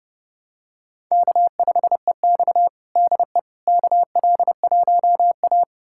Morse code, heard as a keyed tone, about 32 wpm, 700 Hz. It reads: K5EX DE KL1A
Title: USA Callsign DE Callsign - 4 Characters - 30wpm